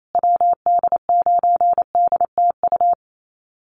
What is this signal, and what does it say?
Morse code, heard as a keyed tone, about 28 wpm, 700 Hz. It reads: WD9DTU
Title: USA Call Signs Repeated 3x 28wpm